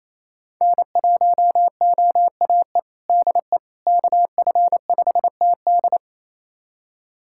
Morse code, heard as a keyed tone, about 28 wpm, 700 Hz. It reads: N1OAE DE KF5TD
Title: USA Callsign DE Callsign - 5 Characters - 28wpm